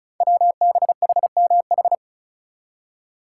Morse code, heard as a keyed tone, about 36 wpm, 700 Hz. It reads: WBHMH
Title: Code Groups - 5 Letter:Numbers - 35wpm